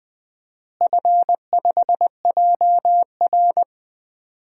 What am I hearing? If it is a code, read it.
Morse code, heard as a keyed tone, about 21 wpm, 700 Hz. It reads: F5JR
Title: DXCC-100 Call Signs - 4 Characters - Repeated 3x - 20wpm